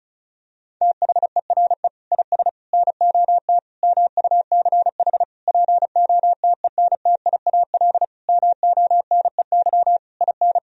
Morse code, heard as a keyed tone, about 36 wpm, 700 Hz. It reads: THERE IS NOT MUCH POTENTIAL MONEY IN
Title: Sentences from Top 1000 Words - Review Part 4 - 35wpm